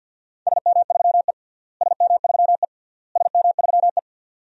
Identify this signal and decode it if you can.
Morse code, heard as a keyed tone, about 49 wpm, 700 Hz. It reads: SM3E SM3E SM3E
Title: DXCC-100 Call Signs - 4 Characters - Repeated 3x - 50wpm